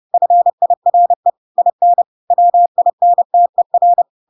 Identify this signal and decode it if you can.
Morse code, heard as a keyed tone, about 30 wpm, 700 Hz. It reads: FIRE IN WINTER
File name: Pinocchio / Pinocchio ch1 30 30 ik1hge